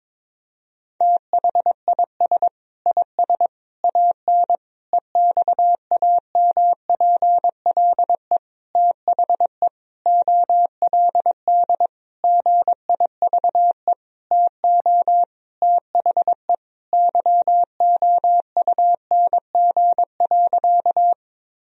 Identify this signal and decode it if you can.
Morse code, heard as a keyed tone, about 22 wpm, 700 Hz. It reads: THIS IS AN EXAMPLE THE OLD GIVE TO THE YOUNG.
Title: Sentences from Top 1000 Words - Review Part 1 - 22wpm